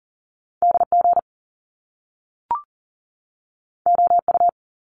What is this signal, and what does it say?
Morse code, heard as a keyed tone, about 42 wpm, 700 Hz. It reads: DG OU
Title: Top Two Letter Combinations in Words 40wpm